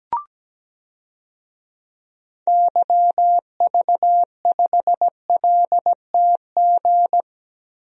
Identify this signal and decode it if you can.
Morse code, heard as a keyed tone, about 17 wpm, 700 Hz. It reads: YV5LTG
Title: DXCC-100 Call Signs - 6 Characters - Repeated 3x - 17wpm